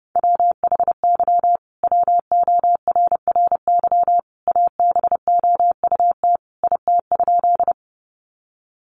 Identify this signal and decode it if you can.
Morse code, heard as a keyed tone, about 30 wpm, 700 Hz. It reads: WHY WORRY ABOUT IT?
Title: Sentences from Top 1100 Words 30wpm